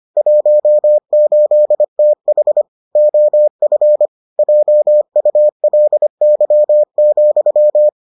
Morse code, heard as a keyed tone, about 25 wpm, 600 Hz. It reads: 18TH OF JULY,